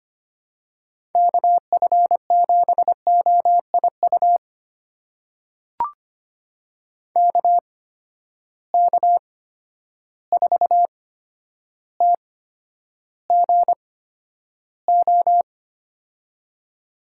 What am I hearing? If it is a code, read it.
Morse code, heard as a keyed tone, about 25 wpm, spaced as for 11 wpm, 700 Hz. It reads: KF7OIU KK4TGO
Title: USA Call Signs - 6 Characters - 8x Character Spacing - 25wpm